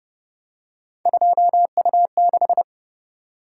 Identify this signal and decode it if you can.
Morse code, heard as a keyed tone, about 33 wpm, 700 Hz. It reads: 2U6